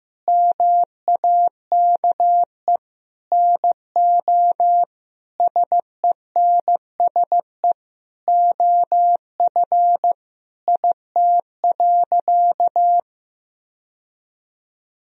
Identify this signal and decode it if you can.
Morse code, heard as a keyed tone, about 15 wpm, 700 Hz. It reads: MAKE NO SENSE OF IT.
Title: Sentences from Top 1000 Words - Review Part 4 - 15wpm